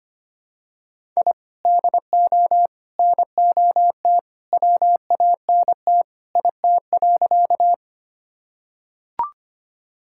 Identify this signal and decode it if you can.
Morse code, heard as a keyed tone, about 25 wpm, 700 Hz. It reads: I DO NOT WANT IT.
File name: practice / 25wpm / Sentences with 2-Word Phrases - 25wpm